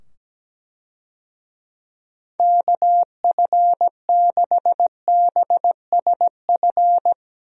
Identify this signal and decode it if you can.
Morse code, heard as a keyed tone, about 17 wpm, 700 Hz. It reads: KF6BSF
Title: USA Call Signs - 6 Characters - 17wpm